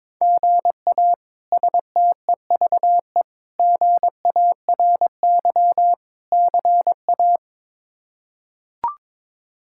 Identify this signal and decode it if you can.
Morse code, heard as a keyed tone, about 22 wpm, 700 Hz. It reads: GA STEVE GARY CA
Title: K1USN SST QSO Practice - 22wpm